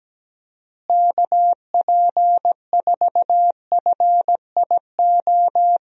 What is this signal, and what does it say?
Morse code, heard as a keyed tone, about 17 wpm, 700 Hz. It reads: KP4FIO
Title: USA Call Signs - 6 Characters - Repeated 3x - 17wpm